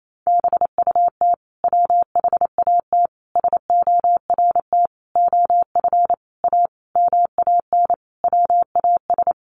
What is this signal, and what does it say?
Morse code, heard as a keyed tone, about 28 wpm, 700 Hz. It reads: BUT WHAT SORT OF A MAN WAS
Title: Sentences from Top 1300 Words 28wpm